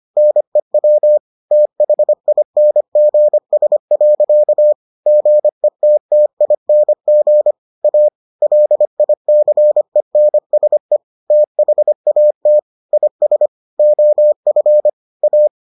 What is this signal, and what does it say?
Morse code, heard as a keyed tone, about 25 wpm, 600 Hz. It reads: NEW THINGS. GETTING A LICENSE THAT IS OF A